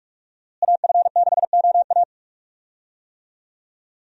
Morse code, heard as a keyed tone, about 42 wpm, 700 Hz. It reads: AF6OA
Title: USA Call Signs - 5 Characters - Repeated 2x - 45wpm